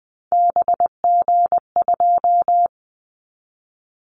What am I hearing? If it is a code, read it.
Morse code, heard as a keyed tone, about 20 wpm, 700 Hz. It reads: BG2
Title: Callsign Prefixes for Top 100 DXCC - 20wpm